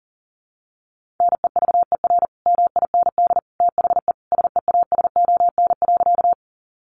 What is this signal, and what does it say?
Morse code, heard as a keyed tone, about 40 wpm, 700 Hz. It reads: NEVER MIND THE SEASON.
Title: Sentences from Top 1400 Words 40wpm